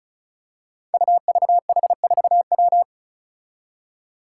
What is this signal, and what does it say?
Morse code, heard as a keyed tone, about 34 wpm, 700 Hz. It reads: UVH4W
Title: Five Challenging Letter-Number Combinations 35wpm